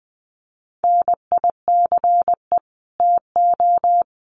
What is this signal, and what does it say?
Morse code, heard as a keyed tone, about 20 wpm, 700 Hz. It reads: NICE TO
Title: Common QSO Phrases 20wpm